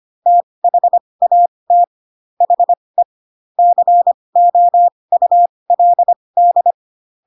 Morse code, heard as a keyed tone, about 25 wpm, spaced as for 20 wpm, 700 Hz. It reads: THAT HE COULD